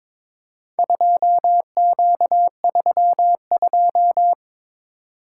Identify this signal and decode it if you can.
Morse code, heard as a keyed tone, about 22 wpm, 700 Hz. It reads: 2Q32
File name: practice / 22wpm / Four Letter-Number Combinations - 22wpm